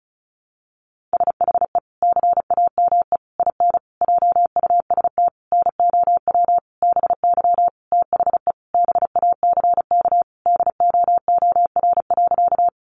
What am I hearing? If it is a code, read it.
Morse code, heard as a keyed tone, about 34 wpm, 700 Hz. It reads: SHE CAME IN JUST NOW BY THE BACK DOOR.
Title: Sentences from Top 300 Words 35wpm